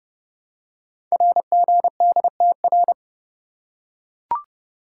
Morse code, heard as a keyed tone, about 29 wpm, 700 Hz. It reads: RGDTR
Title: Code Groups - 5 Letters - 30wpm